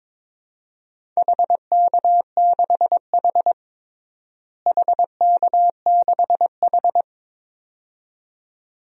Morse code, heard as a keyed tone, about 22 wpm, 700 Hz. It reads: HK6H HK6H
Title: DXCC-100 Call Signs - 4 Characters - Repeated 2x - 22wpm